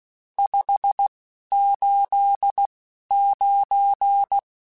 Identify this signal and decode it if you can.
Morse code, heard as a keyed tone, about 16 wpm, spaced as for 11 wpm, 800 Hz. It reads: 589
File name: RSTs / RST-mp3s / Various RST @10